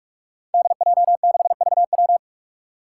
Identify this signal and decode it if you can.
Morse code, heard as a keyed tone, about 44 wpm, 700 Hz. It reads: DJ6VW
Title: Five Challenging Letter-Number Combinations 45wpm